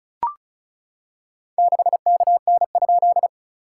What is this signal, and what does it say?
Morse code, heard as a keyed tone, about 36 wpm, 690 Hz. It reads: 6KN?
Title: SOTA QSO Practice 35wpm